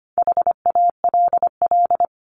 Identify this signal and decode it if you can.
Morse code, heard as a keyed tone, about 25 wpm, 700 Hz. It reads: HALL
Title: Top 1751-2000 Words 25wpm